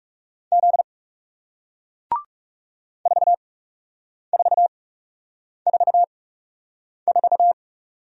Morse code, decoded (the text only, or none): Z 4444